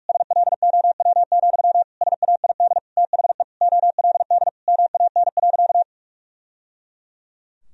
Morse code, heard as a keyed tone, about 46 wpm, 700 Hz. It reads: SLOW, SAID THE OLD MAN.
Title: Sentences from Top 1000 Words - Review Part 2 - 45wpm